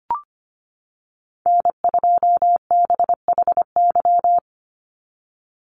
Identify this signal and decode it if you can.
Morse code, heard as a keyed tone, about 25 wpm, 690 Hz. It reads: N2BHY
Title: USA Call Signs 25wpm